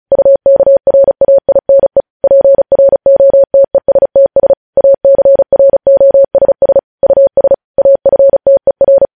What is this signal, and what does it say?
Morse code, heard as a keyed tone, about 36 wpm, 550 Hz. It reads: UKRAINE PROTESTS ACROSS US AFTER
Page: News Headlines in Morse Code at 35 WPM